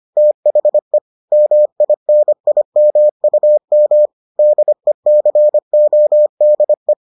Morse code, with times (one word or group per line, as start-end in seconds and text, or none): 0.2-1.0 THE
1.3-4.1 MINIMUM
4.4-6.9 DECODE